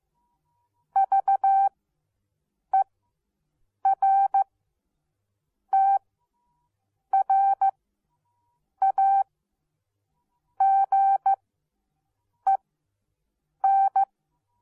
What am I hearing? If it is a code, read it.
Morse code, heard as a keyed tone, about 15 wpm, spaced as for 6 wpm, 780 Hz. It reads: VERTRAGEN